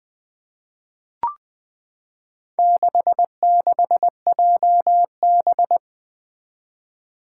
Morse code, heard as a keyed tone, about 20 wpm, 700 Hz. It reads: 66JB
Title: Four Challenging Letter-Number Combinations 20wpm